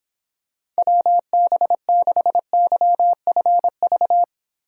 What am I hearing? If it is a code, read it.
Morse code, heard as a keyed tone, about 26 wpm, 700 Hz. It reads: WB6YFV